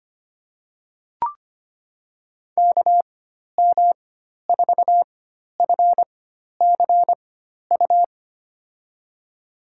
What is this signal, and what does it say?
Morse code, heard as a keyed tone, about 25 wpm, spaced as for 12 wpm, 700 Hz. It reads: KM4FCU
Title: USA Call Signs - 6 Characters - 4x Character Spacing - 25wpm